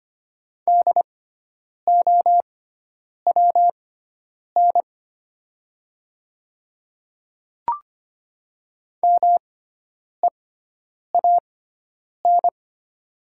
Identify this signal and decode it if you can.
Morse code, heard as a keyed tone, about 25 wpm, spaced as for 9 wpm, 700 Hz. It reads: DOWN MEAN
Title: 6x Character Spacing - Top 200 Words - 25wpm